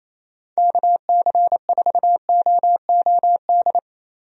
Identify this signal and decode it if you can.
Morse code, heard as a keyed tone, about 28 wpm, 700 Hz. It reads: KC4OOD
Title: USA Call Signs - 6 Characters - Repeated 3x - 28wpm